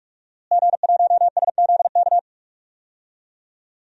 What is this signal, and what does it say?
Morse code, heard as a keyed tone, about 42 wpm, 700 Hz. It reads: G1SZK
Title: DXCC-100 Call Signs - 5 Characters - Repeated 2x - 45wpm